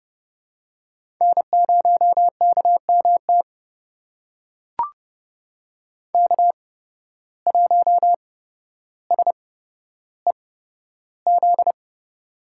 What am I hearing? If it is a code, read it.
Morse code, heard as a keyed tone, about 30 wpm, spaced as for 13 wpm, 700 Hz. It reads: N0KMT K1SEZ
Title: USA Call Signs - 5 Characters - 8x Character Spacing - 30wpm